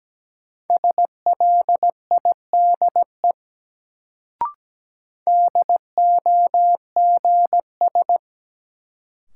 Morse code, heard as a keyed tone, about 17 wpm, 700 Hz. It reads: SLIDE DOGS
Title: Sentences from Top 1600 Words - New Word Drill Alt - 17wpm